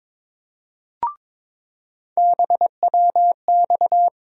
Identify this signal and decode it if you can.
Morse code, heard as a keyed tone, about 22 wpm, 700 Hz. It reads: BWX
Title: USA Call Sign Postfixes - 3 Letters 22wpm